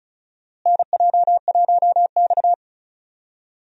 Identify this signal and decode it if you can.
Morse code, heard as a keyed tone, about 36 wpm, 700 Hz. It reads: NJ1X
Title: USA Call Signs - 4 Characters - Repeated 3x - 35wpm